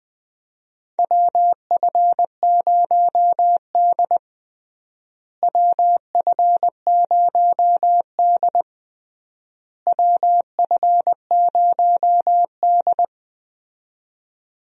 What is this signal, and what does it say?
Morse code, heard as a keyed tone, about 20 wpm, 700 Hz. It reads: WF0D WF0D WF0D
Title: USA Call Signs - 4 Characters - Repeated 3x - 20wpm